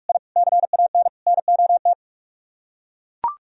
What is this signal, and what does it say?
Morse code, heard as a keyed tone, about 49 wpm, 700 Hz. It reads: I CAN NOT